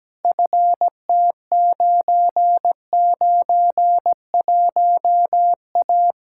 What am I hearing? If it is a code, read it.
Morse code, heard as a keyed tone, about 17 wpm, 700 Hz. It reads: FT991A
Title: Practice QSO - Exchange 2 - Info - 17wpm